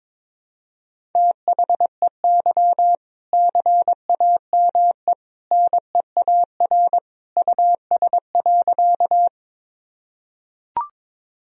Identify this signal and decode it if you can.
Morse code, heard as a keyed tone, about 22 wpm, 700 Hz. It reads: THEY CAME NEAR US.
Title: Sentences from Top 1000 Words - Review Part 1 - 22wpm